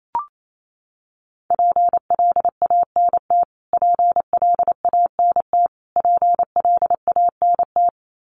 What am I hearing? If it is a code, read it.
Morse code, heard as a keyed tone, about 28 wpm, 700 Hz. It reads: PLANT PLANT PLANT
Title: Top 300 Words Repeated 28wpm